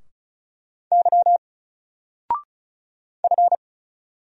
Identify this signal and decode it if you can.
Morse code, heard as a keyed tone, about 35 wpm, 700 Hz. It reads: Y F
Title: Single Letter-Number 35wpm